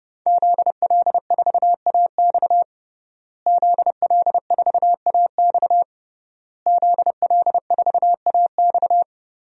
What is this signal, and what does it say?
Morse code, heard as a keyed tone, about 32 wpm, 700 Hz. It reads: ZL4AX ZL4AX ZL4AX
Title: DXCC-100 Call Signs - 5 Characters - Repeated 3x - 30wpm